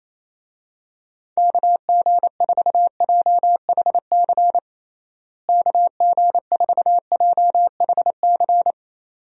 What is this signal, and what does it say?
Morse code, heard as a keyed tone, about 28 wpm, 700 Hz. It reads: KG4JHC KG4JHC
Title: USA Call Signs - 6 Characters - Repeated 2x - 28wpm